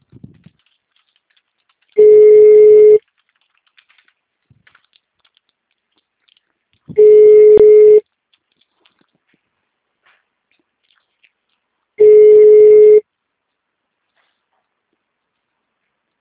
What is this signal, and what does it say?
Ringback tone (Europe, 425 Hz).